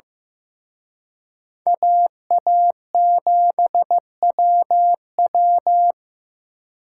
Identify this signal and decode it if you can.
Morse code, heard as a keyed tone, about 18 wpm, 700 Hz. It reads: AA7WW
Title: Speed Racing - USA 2x2 Call Signs 15wpm